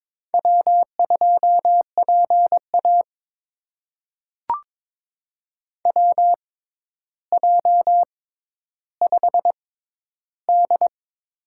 Morse code, heard as a keyed tone, about 22 wpm, spaced as for 11 wpm, 700 Hz. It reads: W2PA WJ5D
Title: USA Call Signs - 4 Characters - 6x Character Spacing - 22wpm